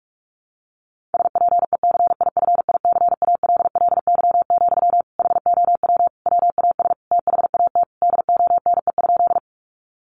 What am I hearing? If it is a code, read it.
Morse code, heard as a keyed tone, about 44 wpm, 700 Hz. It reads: SPECIFICALLY, HOW WAS THAT DONE?
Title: Sentences from Top 1100 Words 45wpm